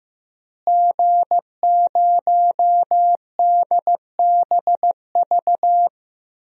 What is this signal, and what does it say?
Morse code, heard as a keyed tone, about 15 wpm, 700 Hz. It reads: G0DBV
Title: DXCC-100 Call Signs - 5 Characters - Repeated 2x - 15wpm